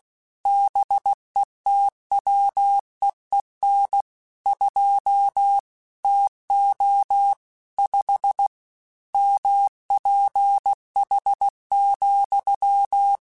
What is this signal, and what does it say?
Morse code, heard as a keyed tone, about 16 wpm, 800 Hz. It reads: BETWEEN 2 TO 5 MPH,
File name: Weather Sets / Weather Sets-mp3s / Weather 5 12 @15